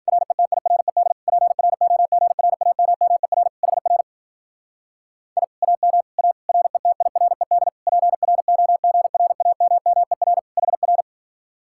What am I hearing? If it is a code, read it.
Morse code, heard as a keyed tone, about 50 wpm, 700 Hz. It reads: RETIRED PROGRAMMER HR I AM A RETIRED PROGRAMMER HR
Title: Speed-Racing Common QSO Phrases - 40wpm